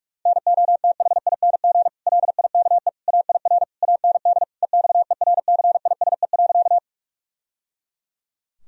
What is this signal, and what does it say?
Morse code, heard as a keyed tone, about 46 wpm, 700 Hz. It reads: NOTHING LIKE AIR AND EXERCISE.
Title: Sentences from Top 1000 Words - Review Part 4 - 45wpm